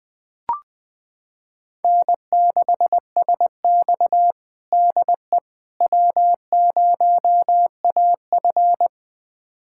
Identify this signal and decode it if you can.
Morse code, heard as a keyed tone, about 21 wpm, 700 Hz. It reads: N6SX DE W0AF
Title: USA Callsign DE Callsign - 4 Characters - 20wpm